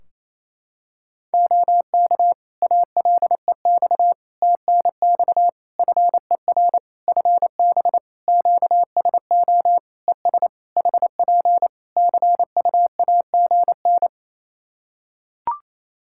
Morse code, heard as a keyed tone, about 28 wpm, 700 Hz. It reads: OK ALEX TNX FER FB QSO ES HP CUAGN
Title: Practice QSO - Exchange 4 - Ending - 28wpm